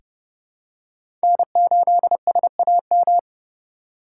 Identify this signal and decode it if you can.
Morse code, heard as a keyed tone, about 32 wpm, 700 Hz. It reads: N8SAM
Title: USA Call Signs - 5 Characters - 4x Character Spacing - 30wpm